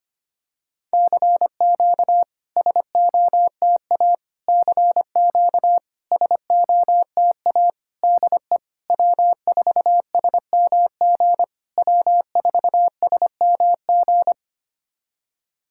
Morse code, heard as a keyed tone, about 25 wpm, 700 Hz. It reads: CQ SOTA CQ SOTA DE W4SMG W4SMG
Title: SOTA QSO Practice 25wpm